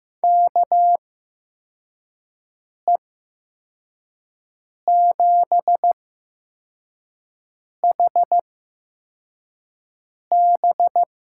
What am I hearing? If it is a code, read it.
Morse code, heard as a keyed tone, about 15 wpm, spaced as for 4 wpm, 700 Hz. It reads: KE7HB
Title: USA Call Signs - 6 Characters - 8x Character Spacing - 15wpm